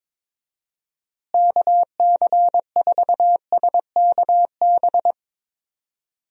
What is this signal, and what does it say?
Morse code, heard as a keyed tone, about 22 wpm, 700 Hz. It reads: KC4SKB
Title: USA Call Signs - 6 Characters - Repeated 2x - 22wpm